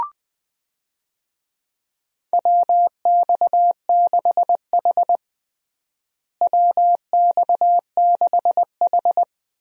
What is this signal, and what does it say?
Morse code, heard as a keyed tone, about 20 wpm, 700 Hz. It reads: WX6H WX6H
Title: USA Call Signs - 4 Characters - Repeated 2x - 20wpm